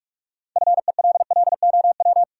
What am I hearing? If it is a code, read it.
Morse code, heard as a keyed tone, about 46 wpm, 700 Hz. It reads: FELLOW
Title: Sets of 5 Words Encore - New Word Drill - 45wpm